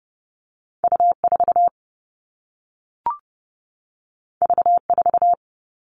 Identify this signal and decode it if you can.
Morse code, heard as a keyed tone, about 30 wpm, 700 Hz. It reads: U4 V4
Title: All Cut Numbers - 2-Digit 30wpm